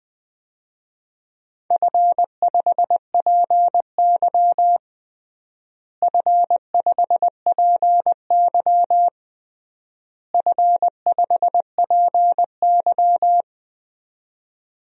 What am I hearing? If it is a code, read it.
Morse code, heard as a keyed tone, about 20 wpm, 700 Hz. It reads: F5PY F5PY F5PY
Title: DXCC-100 Call Signs - 4 Characters - Repeated 3x - 20wpm